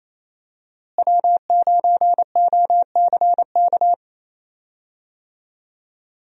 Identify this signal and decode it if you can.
Morse code, heard as a keyed tone, about 28 wpm, 700 Hz. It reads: W9OCK
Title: USA Call Signs - 5 Characters - Repeated 2x - 28wpm